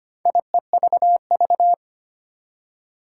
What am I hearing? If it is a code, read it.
Morse code, heard as a keyed tone, about 25 wpm, 700 Hz. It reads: IEVV
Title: Four Challenging Letter-Number Combinations 25wpm